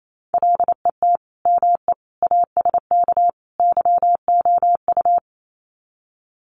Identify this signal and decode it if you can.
Morse code, heard as a keyed tone, about 28 wpm, 700 Hz. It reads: LET ME ASK YOU
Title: Sets of 4 Words 28wpm